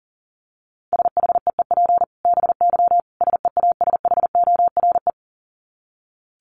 Morse code, heard as a keyed tone, about 40 wpm, 700 Hz. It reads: SHEEP BY SEASHORE
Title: Dit Challenge 40wpm